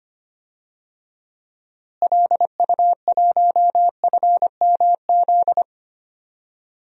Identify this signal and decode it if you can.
Morse code, heard as a keyed tone, about 25 wpm, 700 Hz. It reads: LU1FMZ
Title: DXCC-100 Call Signs - 6 Characters - Repeated 3x - 25wpm